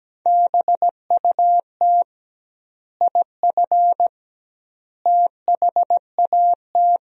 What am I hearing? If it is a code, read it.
Morse code, heard as a keyed tone, about 17 wpm, 700 Hz. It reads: BUT IF THAT
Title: 2x Word-Spacing - Sentences from Top 200 Words - 17wpm